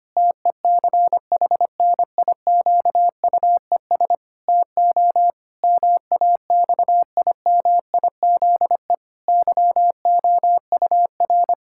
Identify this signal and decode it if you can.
Morse code, heard as a keyed tone, about 25 wpm, 700 Hz. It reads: TECHNIQUES TO MAXIMIZE YOUR
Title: Pangram Challenge v2 - 25wpm